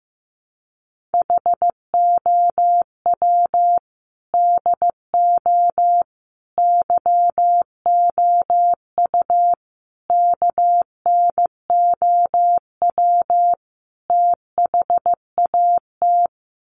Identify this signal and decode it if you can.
Morse code, heard as a keyed tone, about 15 wpm, 700 Hz. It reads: HOW DO YOU KNOW THAT
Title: Sets of 5 Words Encore 15wpm